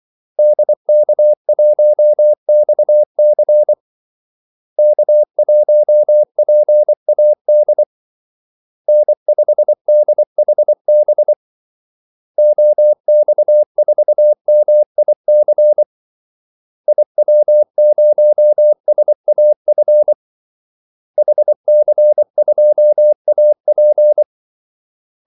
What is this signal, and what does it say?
Morse code, heard as a keyed tone, about 24 wpm, 600 Hz. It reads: DK1XC K1PAD N5DHB OX4MIC IW0SAF HC2AP
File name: Callsigns-24wpm